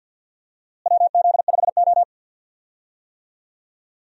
Morse code, heard as a keyed tone, about 52 wpm, 700 Hz. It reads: WZ5O
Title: USA Call Signs - 4 Characters - Repeated 3x - 50wpm